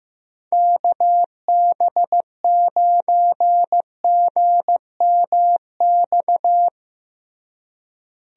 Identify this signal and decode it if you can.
Morse code, heard as a keyed tone, about 15 wpm, 700 Hz. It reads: KB9GMX
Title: USA Call Signs - 6 Characters - Repeated 3x - 15wpm